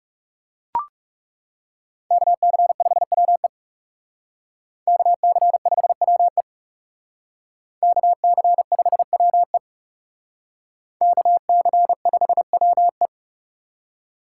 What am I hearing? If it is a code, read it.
Morse code, heard as a keyed tone, about 38 wpm, 700 Hz. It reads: KC5WE KC5WE KC5WE KC5WE